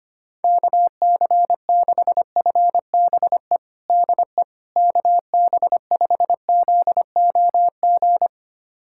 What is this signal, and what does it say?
Morse code, heard as a keyed tone, about 25 wpm, 700 Hz. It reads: KC6FBE DE KB5ZOG
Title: USA Callsign DE Callsign - 6 Characters - 25wpm